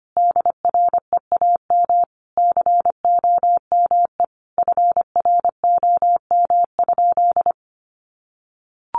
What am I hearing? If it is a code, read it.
Morse code, heard as a keyed tone, about 25 wpm, 700 Hz. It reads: DREAM COME FROM?
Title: Sentences from Top 1500 Words 25wpm